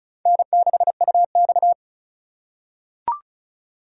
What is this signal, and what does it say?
Morse code, heard as a keyed tone, about 36 wpm, 700 Hz. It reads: N6UX
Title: USA Call Signs - 4 Characters - 3x Character Spacing - 35wpm